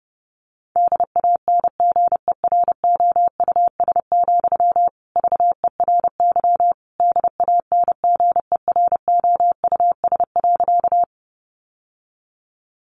Morse code, heard as a keyed tone, about 30 wpm, 700 Hz. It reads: DANGEROUS, VERY DANGEROUS.
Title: Sentences from Top 1200 Words 30wpm